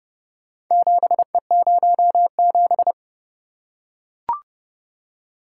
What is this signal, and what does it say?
Morse code, heard as a keyed tone, about 29 wpm, 700 Hz. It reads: 7E07